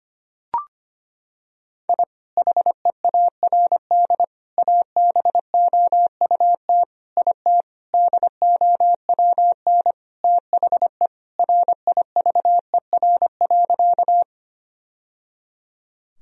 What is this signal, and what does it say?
Morse code, heard as a keyed tone, about 25 wpm, 700 Hz. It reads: I HEARD ABOUT IT DOWN THE RIVER.
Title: Sentences from Top 1000 Words - Review Part 1 - 25wpm